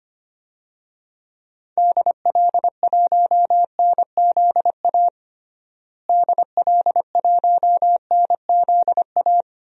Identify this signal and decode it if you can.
Morse code, heard as a keyed tone, about 25 wpm, 700 Hz. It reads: DL1NZA DL1NZA
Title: DXCC-100 Call Signs - 6 Characters - Repeated 3x - 25wpm